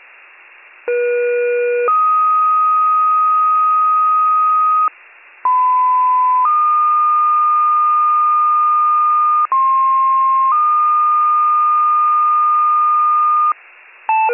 Two-tone page, heard as a one-second tone, then a three-second tone, about 480 then 1250 Hz, about 1010 then 1220 Hz, about 1060 then 1220 Hz. It is heard from 0.9 to 13.5 s.